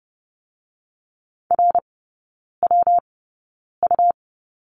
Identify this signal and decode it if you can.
Morse code, heard as a keyed tone, about 30 wpm, 700 Hz. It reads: R W U
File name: Lesson 19e - Letter w - Three Characters 30wpm